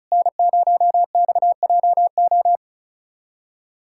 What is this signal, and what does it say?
Morse code, heard as a keyed tone, about 36 wpm, 700 Hz. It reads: N0XJO
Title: USA Callsign DE Callsign - 5 Characters - 35wpm